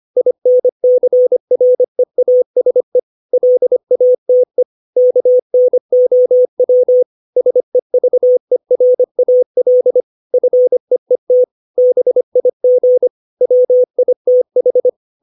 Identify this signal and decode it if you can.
Morse code, heard as a keyed tone, about 25 wpm, 500 Hz. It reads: INCREASE LATE KNOW SEVERAL FEET BIG WITH
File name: 632 words / 632 Words 25WPM